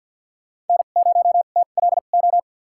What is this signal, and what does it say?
Morse code, heard as a keyed tone, about 47 wpm, spaced as for 35 wpm, 700 Hz. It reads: N0TLO
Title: USA Call Signs - 2x Character Spacing - 50wpm